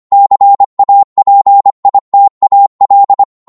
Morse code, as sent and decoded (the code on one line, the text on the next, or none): -.-. .- .--. .. - .- .-..
CAPITAL